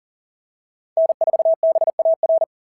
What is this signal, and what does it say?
Morse code, heard as a keyed tone, about 37 wpm, 660 Hz. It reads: N4BAR
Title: SOTA QSO Practice 40wpm